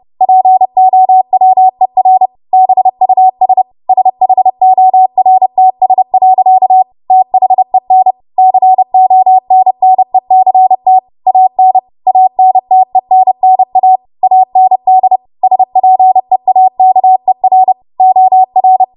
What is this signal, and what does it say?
Morse code, heard as a keyed tone, about 30 wpm, 750 Hz. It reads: POWER BUS SHORTS. THEN CONNECT AN ANTENNA AND SPEAKER OR
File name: Archive / 30 WPM / 30WPM